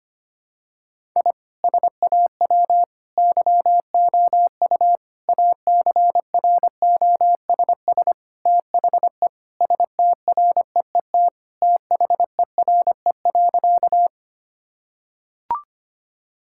Morse code, heard as a keyed tone, about 25 wpm, 700 Hz. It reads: I SAW YOU ACROSS THE STREET THERE.
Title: Sentences from Top 1000 Words - Review Part 5 - 25wpm